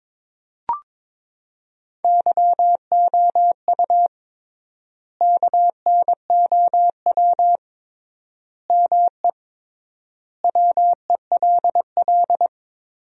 Morse code, heard as a keyed tone, about 22 wpm, 700 Hz. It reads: YOU KNOW ME WELL
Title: 3x Word-Spacing - Sentences from Top 400 Words - 22wpm